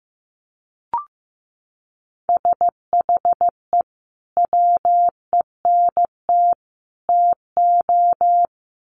Morse code, heard as a keyed tone, about 15 wpm, 700 Hz. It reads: SHE WENT TO
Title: Sentences from Top 900 Words 15wpm